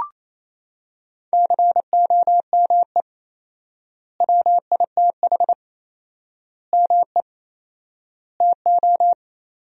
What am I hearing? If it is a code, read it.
Morse code, heard as a keyed tone, about 27 wpm, 700 Hz. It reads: COME WITH ME TO
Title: 4x Word-Spacing - Sentences from Top 200 Words - 28wpm